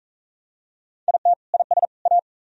Morse code, heard as a keyed tone, about 41 wpm, 700 Hz. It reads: IT IS A